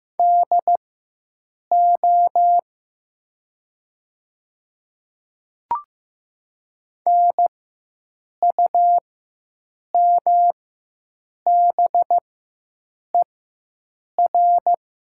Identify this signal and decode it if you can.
Morse code, heard as a keyed tone, about 15 wpm, spaced as for 7 wpm, 700 Hz. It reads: DO NUMBER
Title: 4x Character Spacing - Top 100 Words - 15wpm